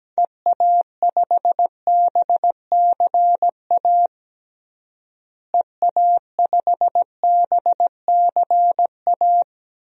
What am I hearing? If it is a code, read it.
Morse code, heard as a keyed tone, about 17 wpm, 700 Hz. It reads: EA5BCA EA5BCA
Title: DXCC-100 Call Signs - 6 Characters - Repeated 2x - 17wpm